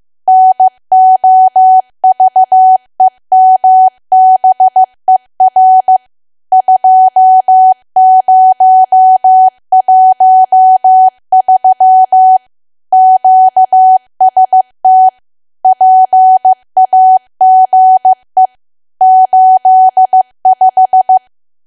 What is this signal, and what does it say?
Morse code, heard as a keyed tone, about 15 wpm, 750 Hz. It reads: NOVEMBER 2013 QST PAGE 85
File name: cwa / W1AW-Level2 / 15WPM